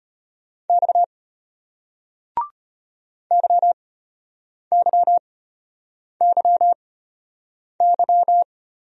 Morse code, heard as a keyed tone, about 35 wpm, spaced as for 8 wpm, 700 Hz. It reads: X YYYY